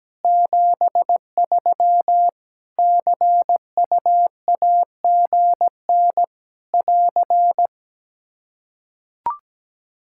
Morse code, heard as a keyed tone, about 17 wpm, 700 Hz. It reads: 73 CUAGN <AR>
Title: Practice QSO - Exchange 4 - Ending - 17wpm